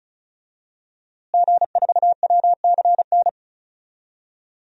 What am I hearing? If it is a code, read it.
Morse code, heard as a keyed tone, about 36 wpm, 700 Hz. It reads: G4WCN
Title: DXCC-100 Call Signs - 5 Characters - Repeated 2x - 35wpm